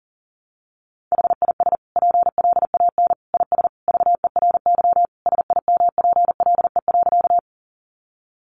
Morse code, heard as a keyed tone, about 40 wpm, 700 Hz. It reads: HIS PLAN IS VERY SIMPLE.
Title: Sentences from Top 500 Words 40wpm